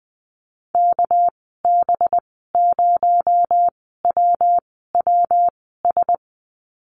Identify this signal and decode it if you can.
Morse code, heard as a keyed tone, about 20 wpm, spaced as for 15 wpm, 700 Hz. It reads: KB0WWS
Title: 219 Lesson 35g - Number 0 - Call Signs 20wpm